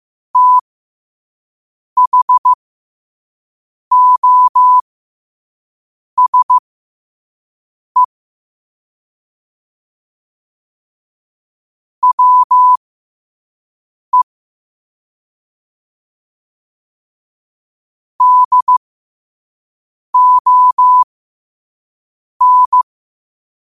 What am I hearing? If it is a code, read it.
Morse code, heard as a keyed tone, about 15 wpm, spaced as for 5 wpm, 1000 Hz. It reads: THOSE WE DON
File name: qotdfiles / 5wpm